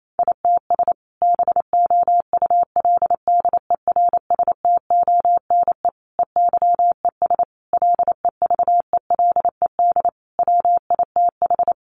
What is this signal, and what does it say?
Morse code, heard as a keyed tone, about 28 wpm, 700 Hz. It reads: ITS BOULDERSTONE EYES LEVELED WITH
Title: A Sound Of Thunder 28wpm